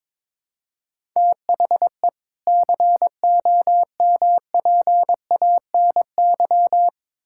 Morse code, heard as a keyed tone, about 22 wpm, 700 Hz. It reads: THE COMPANY
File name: Build A Phrase - 2-Word Phrases - 22wpm